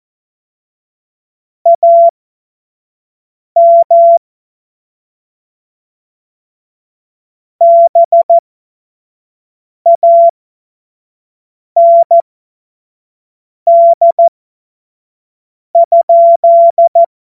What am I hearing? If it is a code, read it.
Morse code, heard as a keyed tone, about 14 wpm, spaced as for 5 wpm, 680 Hz. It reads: AM BAND?
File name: Morsekurs / Witze / Witz 22. November 2022 25 F70